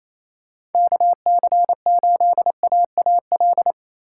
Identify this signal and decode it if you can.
Morse code, heard as a keyed tone, about 28 wpm, 700 Hz. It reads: KC8AAL
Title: USA Call Signs - 6 Characters - 3x Character Spacing - 28wpm